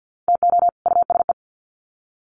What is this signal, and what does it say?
Morse code, heard as a keyed tone, about 51 wpm, 700 Hz. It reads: TO USE
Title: Sets of 2 Words 50wpm